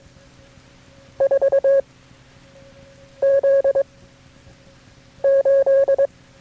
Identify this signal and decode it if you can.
Morse code, heard as a keyed tone, about 22 wpm, 550 Hz. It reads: 4 Z 8
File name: FreeFiles / CW-practice / KB6MT-tape3-side1